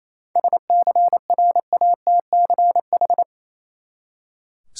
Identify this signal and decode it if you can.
Morse code, heard as a keyed tone, about 28 wpm, 700 Hz. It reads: SCRATCH
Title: Seven Letter Words - 28wpm